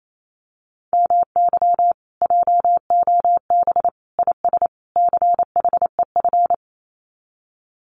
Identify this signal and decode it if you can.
Morse code, heard as a keyed tone, about 28 wpm, 700 Hz. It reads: MY JOB IS CHEF
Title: Common QSO Phrases Encore 28wpm